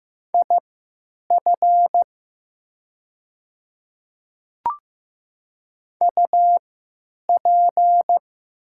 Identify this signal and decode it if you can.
Morse code, heard as a keyed tone, about 15 wpm, spaced as for 9 wpm, 700 Hz. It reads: IF UP
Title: 3x Character Spacing - Top 300 Words - 15wpm